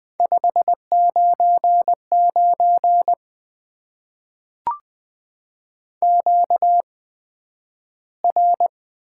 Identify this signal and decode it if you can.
Morse code, heard as a keyed tone, about 20 wpm, spaced as for 11 wpm, 700 Hz. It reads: 599 QR
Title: Top 90 QSO Elements - 8x Character Spacing - 20wpm